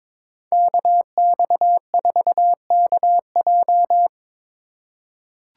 Morse code, heard as a keyed tone, about 22 wpm, 700 Hz. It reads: KX4KJ
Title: USA Call Signs - 5 Characters - 3x Character Spacing - 22wpm